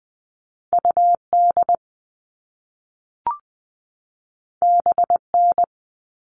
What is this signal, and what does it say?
Morse code, heard as a keyed tone, about 20 wpm, 700 Hz. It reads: UD BN
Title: All Cut Numbers - 2-Digit 20wpm